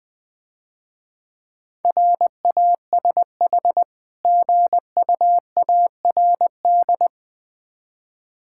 Morse code, heard as a keyed tone, about 20 wpm, 690 Hz. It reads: RASH GUARD
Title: Difficult Words - Speed-Racing with Hints - 20wpm